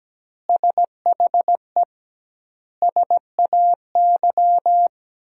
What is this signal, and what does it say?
Morse code, heard as a keyed tone, about 17 wpm, 700 Hz. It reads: SHE SAY
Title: 2x Word-Spacing - Sentences from Top 200 Words - 17wpm